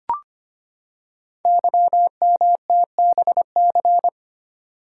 Morse code, heard as a keyed tone, about 25 wpm, 700 Hz. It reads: YMTBC
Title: Code Groups - 5 Letters - 25wpm